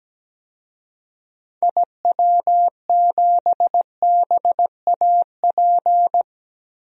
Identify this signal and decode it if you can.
Morse code, heard as a keyed tone, about 17 wpm, 700 Hz. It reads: IW7BAP
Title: DXCC-100 Call Signs - 6 Characters - Repeated 3x - 17wpm